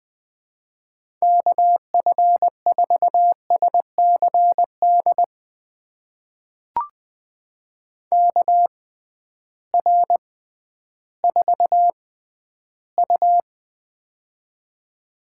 Morse code, heard as a keyed tone, about 20 wpm, spaced as for 12 wpm, 700 Hz. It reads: KF4SCD KR4U
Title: USA Call Signs - 6x Character Spacing - 20wpm